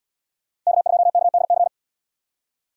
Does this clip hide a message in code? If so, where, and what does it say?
Yes: Morse code from 0.7 to 1.7 s, heard as a keyed tone, about 77 wpm, 700 Hz. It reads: K2KDC